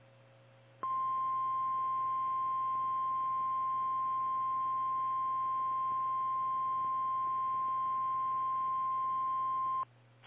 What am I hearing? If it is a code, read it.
Emergency Alert System (SAME) tones (the 1050 Hz attention tone). It is heard from 0.8 to 9.9 s.
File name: eas / KKPRKXPN / KKPRKXPN